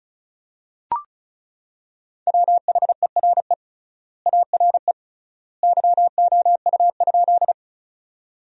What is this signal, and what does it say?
Morse code, heard as a keyed tone, about 34 wpm, 700 Hz. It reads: WHERE ARE YOU?
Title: 3x Word-Spacing - Sentences from Top 200 Words - 35wpm